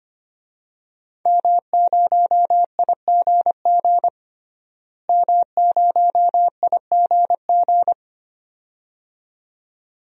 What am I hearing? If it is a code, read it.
Morse code, heard as a keyed tone, about 25 wpm, 700 Hz. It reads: M0IGG M0IGG
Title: DXCC-100 Call Signs - 5 Characters - Repeated 2x - 25wpm